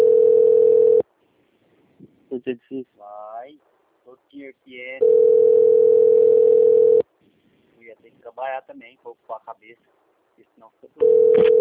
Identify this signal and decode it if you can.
Ringback tone (North America), 440 + 480 Hz, with speech.
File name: rg-1600-unknown- - - .8961